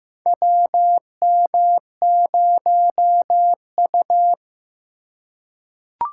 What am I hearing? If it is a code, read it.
Morse code, heard as a keyed tone, about 15 wpm, 700 Hz. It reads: WM0U
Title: USA Call Signs - 4 Characters - 3x Character Spacing - 15wpm